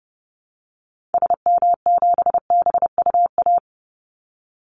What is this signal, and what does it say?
Morse code, heard as a keyed tone, about 30 wpm, 700 Hz. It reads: SM7BUA
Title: Call Signs Repeated 2x - 30wpm